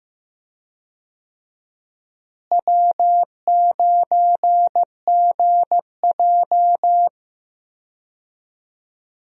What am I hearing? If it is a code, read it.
Morse code, heard as a keyed tone, about 15 wpm, 700 Hz. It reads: W9GJ
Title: USA Call Signs - 4 Characters - Repeated 3x - 15wpm